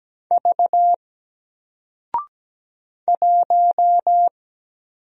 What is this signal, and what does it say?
Morse code, heard as a keyed tone, about 17 wpm, 700 Hz. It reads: V 1
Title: Single Letter-Number 17wpm